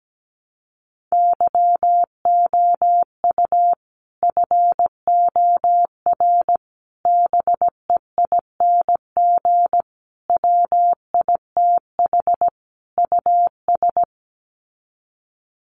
Morse code, heard as a keyed tone, about 17 wpm, 700 Hz. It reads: YOU FOR BEING WITH US
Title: Sets of 5 Words 17wpm